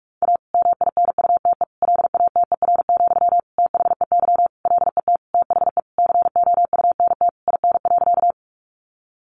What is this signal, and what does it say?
Morse code, heard as a keyed tone, about 44 wpm, 700 Hz. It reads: A MINUTE LATER, THEY LET THE COUNT IN.
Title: Sentences from Top 1500 Words 45wpm